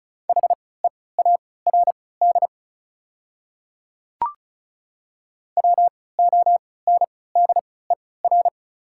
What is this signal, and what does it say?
Morse code, heard as a keyed tone, about 36 wpm, spaced as for 20 wpm, 690 Hz. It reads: HEARD WONDER
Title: 3x Character Spacing - Top 500 Words - 35wpm